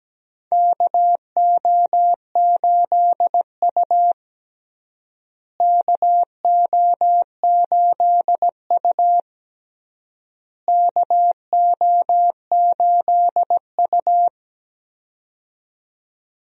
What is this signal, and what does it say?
Morse code, heard as a keyed tone, about 17 wpm, 700 Hz. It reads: KO8U KO8U KO8U
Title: USA Call Signs - 4 Characters - Repeated 3x - 17wpm